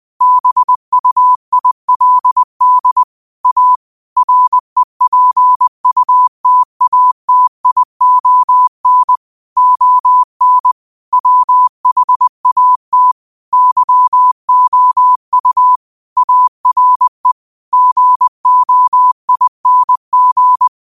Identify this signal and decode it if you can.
Morse code, heard as a keyed tone, about 20 wpm, 1000 Hz. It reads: BUILD A REPUTATION ON WHAT YOU ARE GOING